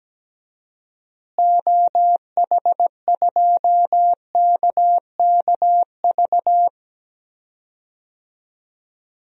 Morse code, heard as a keyed tone, about 17 wpm, 700 Hz. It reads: OH2KKV
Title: DXCC-100 Call Signs - 6 Characters - Repeated 3x - 17wpm